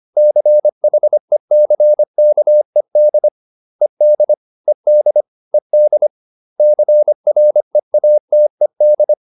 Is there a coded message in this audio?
Yes: Morse code from 0.2 to 9.1 s, heard as a keyed tone, about 25 wpm, 600 Hz. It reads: CHECKED ED ED ED CREATED